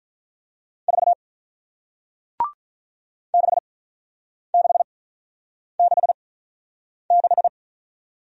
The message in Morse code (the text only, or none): 4 6666